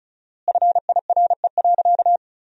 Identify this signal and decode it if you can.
Morse code, heard as a keyed tone, about 36 wpm, 700 Hz. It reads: FIRE.
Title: 4x Word-Spacing - Sentences from Top 400 Words - 35wpm